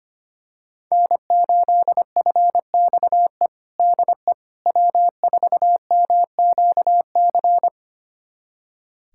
Morse code, heard as a keyed tone, about 25 wpm, 700 Hz. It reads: N8FXE DE W4MQC
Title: USA Callsign DE Callsign 25wpm